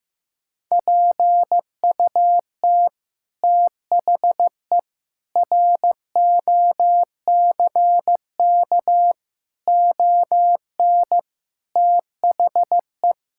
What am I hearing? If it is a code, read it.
Morse code, heard as a keyed tone, about 15 wpm, 700 Hz. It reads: PUT THE ROCK ON THE
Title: Sentences from Top 1000 Words - Review Part 6 - 15wpm